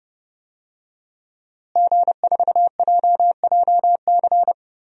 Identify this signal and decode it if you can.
Morse code, heard as a keyed tone, about 30 wpm, 700 Hz. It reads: G4JJC
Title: DXCC-100 Call Signs - 5 Characters - Repeated 2x - 30wpm